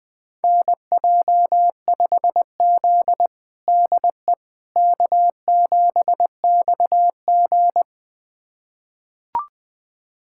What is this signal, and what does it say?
Morse code, heard as a keyed tone, about 20 wpm, 700 Hz. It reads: NJ5Z DE K7XG
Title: USA Callsign DE Callsign - 4 Characters - 20wpm